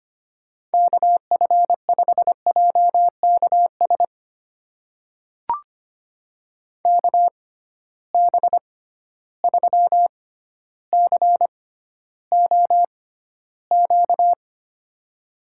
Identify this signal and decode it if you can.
Morse code, heard as a keyed tone, about 25 wpm, spaced as for 13 wpm, 700 Hz. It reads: KF5JKS KB3COQ
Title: USA Call Signs - 6 Characters - 6x Character Spacing - 25wpm